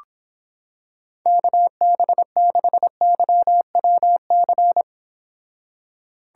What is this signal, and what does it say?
Morse code, heard as a keyed tone, about 26 wpm, 700 Hz. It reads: KB6YWC